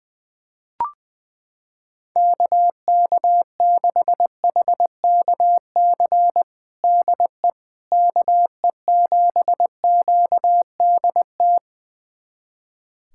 Morse code, heard as a keyed tone, about 21 wpm, 700 Hz. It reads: KK6HKC DE KE7QDT
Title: USA Callsign DE Callsign - 6 Characters - 20wpm